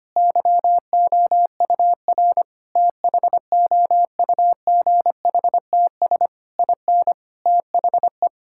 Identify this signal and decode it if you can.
Morse code, heard as a keyed tone, about 25 wpm, 700 Hz. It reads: YOUR THOUGHTS IN THE
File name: practice / 25wpm / Sentences with 9-Word Phrases - 25wpm